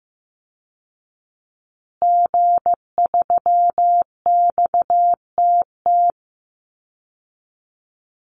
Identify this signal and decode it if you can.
Morse code, heard as a keyed tone, about 15 wpm, 700 Hz. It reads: G3XTT
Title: Call Signs Repeated 2x - 15wpm